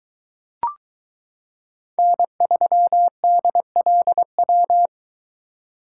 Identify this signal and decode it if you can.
Morse code, heard as a keyed tone, about 22 wpm, 700 Hz. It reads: N3DLW